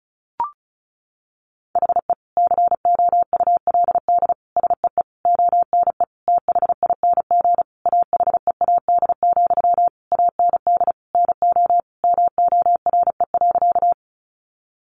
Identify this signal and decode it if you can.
Morse code, heard as a keyed tone, about 34 wpm, 700 Hz. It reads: HE COULD SEE ONE THING AHEAD, AND NO MORE.
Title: Sentences from Top 1100 Words 35wpm